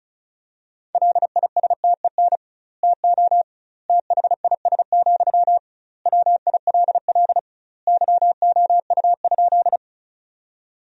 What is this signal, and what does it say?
Morse code, heard as a keyed tone, about 36 wpm, 700 Hz. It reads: LISTEN TO THIS, WILL YOU?
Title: 2x Word-Spacing - Sentences from Top 400 Words - 35wpm